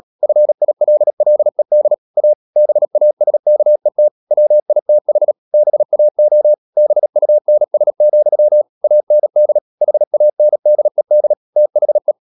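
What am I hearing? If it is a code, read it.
Morse code, heard as a keyed tone, about 37 wpm, 600 Hz. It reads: FILLED A BASKET WITH BAO BUNS, AND HANDED THE
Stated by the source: Morse code practice at 35–39 wpm sending English text of Ken Follett's novel Never, from CW med Gnister.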